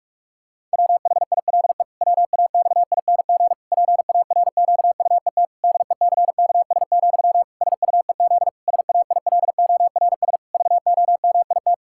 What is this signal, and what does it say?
Morse code, heard as a keyed tone, about 48 wpm, 700 Hz. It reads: WHILE WAXING PARQUET DECKS, SUEZ SAILORS VOMIT
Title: Pangram Challenge v2 - 45wpm